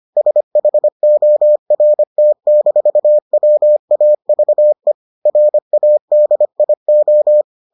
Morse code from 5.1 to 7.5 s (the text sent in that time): RADIO